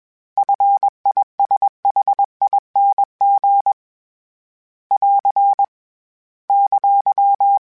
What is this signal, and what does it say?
Morse code, heard as a keyed tone, about 21 wpm, 800 Hz. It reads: FISHING <AR> !
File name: Short Runs / Short Runs-mp3s / SinglePrefixes @20